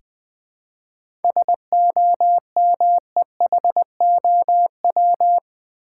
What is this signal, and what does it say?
Morse code, heard as a keyed tone, about 20 wpm, 700 Hz. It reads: SOMEHOW
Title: Seven Letter Words - 20wpm